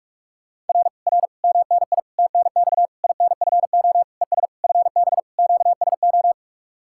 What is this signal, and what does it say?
Morse code, heard as a keyed tone, about 49 wpm, 700 Hz. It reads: R R MNI TNX INFO ES FB QSO